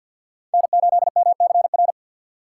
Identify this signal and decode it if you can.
Morse code, heard as a keyed tone, about 53 wpm, 700 Hz. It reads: N8MCR
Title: USA Call Signs - 5 Characters - 2x Character Spacing - 50wpm